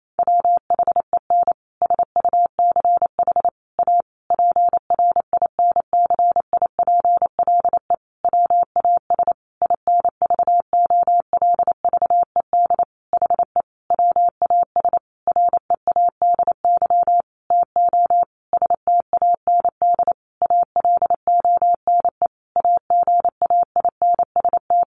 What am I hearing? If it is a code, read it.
Morse code, heard as a keyed tone, about 28 wpm, 700 Hz. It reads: WHEN SUCH A PRINCIPLE WAS INVOLVED HE WAS READY TO STAND ALONE AGAINST
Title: Sentences from Top 1400 Words 28wpm